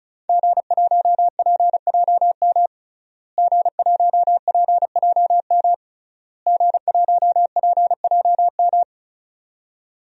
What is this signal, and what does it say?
Morse code, heard as a keyed tone, about 34 wpm, 700 Hz. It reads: G1PJM G1PJM G1PJM
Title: DXCC-100 Call Signs - 5 Characters - Repeated 3x - 35wpm